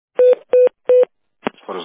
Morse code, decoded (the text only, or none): none